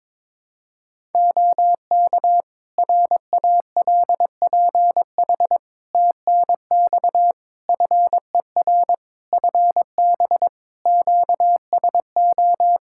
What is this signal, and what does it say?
Morse code, heard as a keyed tone, about 22 wpm, 700 Hz. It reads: OK RALPH TNX FER FB QSO
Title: Practice QSO - Exchange 4 - Ending - 22wpm